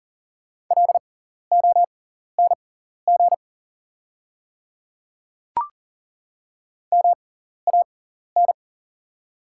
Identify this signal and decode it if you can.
Morse code, heard as a keyed tone, about 40 wpm, spaced as for 14 wpm, 700 Hz. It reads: LONG MAN
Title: 6x Character Spacing - Top 200 Words - 40wpm